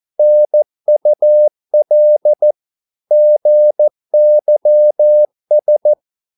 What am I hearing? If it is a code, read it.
Morse code, heard as a keyed tone, about 14 wpm, 600 Hz. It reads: NUL GYS
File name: Korte ord DK 14wpm